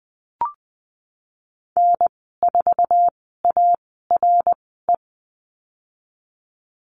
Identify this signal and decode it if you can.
Morse code, heard as a keyed tone, about 20 wpm, spaced as for 14 wpm, 700 Hz. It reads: N4ARE
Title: 062 Lesson 11g - Letter h - Call Signs 20wpm